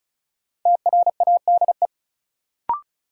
Morse code, heard as a keyed tone, about 36 wpm, 700 Hz. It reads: TRADE
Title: Sets of 5 Words Encore - New Word Drill - 35wpm